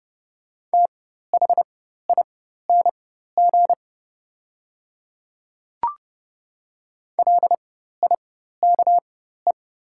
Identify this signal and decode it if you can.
Morse code, heard as a keyed tone, about 30 wpm, spaced as for 14 wpm, 700 Hz. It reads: THING LIKE
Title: 4x Character Spacing - Top 100 Words - 30wpm